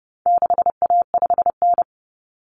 Morse code, heard as a keyed tone, about 30 wpm, 700 Hz. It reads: 6A5N
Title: Common Cut Numbers - 4-Digit 30wpm